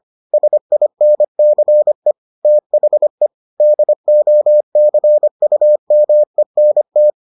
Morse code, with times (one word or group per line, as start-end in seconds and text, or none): 0.3-2.1 SINCE
2.5-3.3 THE
3.6-7.1 DOCUMENT